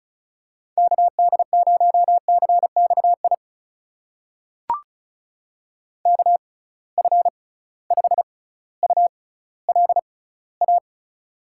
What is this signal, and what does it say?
Morse code, heard as a keyed tone, about 36 wpm, spaced as for 18 wpm, 700 Hz. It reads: KD0CXI KF5ULA
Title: USA Call Signs - 6 Characters - 6x Character Spacing - 35wpm